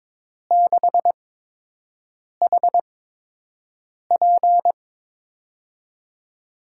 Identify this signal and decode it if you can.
Morse code, heard as a keyed tone, about 22 wpm, spaced as for 6 wpm, 700 Hz. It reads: 6HP
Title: USA Call Signs - 5 Characters - 8x Character Spacing - 22wpm